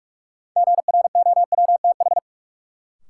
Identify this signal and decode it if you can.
Morse code, heard as a keyed tone, about 46 wpm, 700 Hz. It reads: GROWTH
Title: Sentences from Top 1000 Words - Review Part 5 - New Word Drill - 45wpm